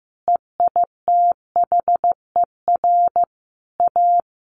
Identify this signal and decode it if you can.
Morse code, heard as a keyed tone, about 15 wpm, 700 Hz. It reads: EITHER A
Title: Sentences from Top 600 Words 15wpm